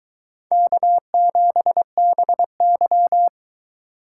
Morse code, heard as a keyed tone, about 22 wpm, 700 Hz. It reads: K7BY